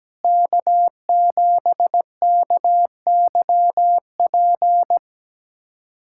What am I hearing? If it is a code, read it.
Morse code, heard as a keyed tone, about 17 wpm, 700 Hz. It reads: K7KYP
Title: USA Callsign DE Callsign - 5 Characters - 17wpm